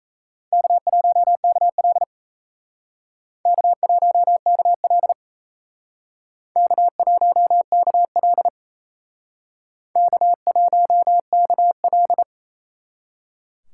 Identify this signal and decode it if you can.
Morse code, heard as a keyed tone, about 36 wpm, 700 Hz. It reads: K1KL K1KL K1KL K1KL